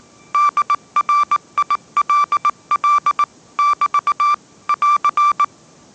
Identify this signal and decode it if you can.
Morse code, heard as a keyed tone, about 19 wpm, 1170 Hz. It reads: DRILL <BT> <AR>
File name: 20wpm